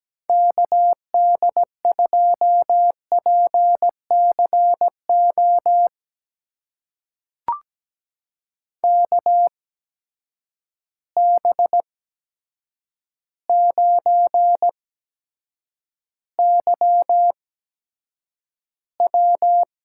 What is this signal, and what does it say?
Morse code, heard as a keyed tone, about 17 wpm, spaced as for 8 wpm, 700 Hz. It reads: KD2PCO KB9YW
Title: USA Call Signs - 6 Characters - 8x Character Spacing - 17wpm